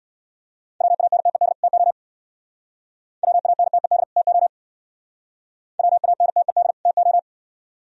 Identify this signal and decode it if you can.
Morse code, heard as a keyed tone, about 61 wpm, 700 Hz. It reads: WANTED TO WANTED TO WANTED TO
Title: Speed-Racing - Sets of 2 Words 50wpm